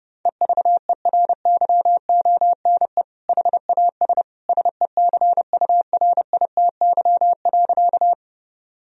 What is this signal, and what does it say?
Morse code, heard as a keyed tone, about 30 wpm, 700 Hz. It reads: EVERYONE HAS SECURITY.
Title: Sentences from Top 1000 Words - Review Part 4 - 30wpm